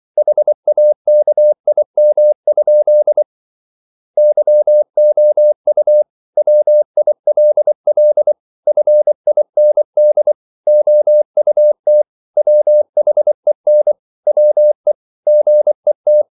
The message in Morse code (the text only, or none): HAKIM? YOU WILL FIND OUT WHEN WE GET